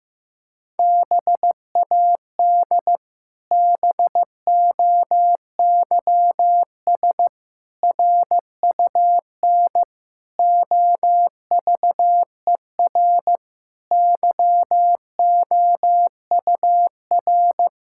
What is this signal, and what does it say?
Morse code, heard as a keyed tone, about 15 wpm, 700 Hz. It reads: BAD BOYS RUN OVER YOUR
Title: Pangram Challenge v2 - 15wpm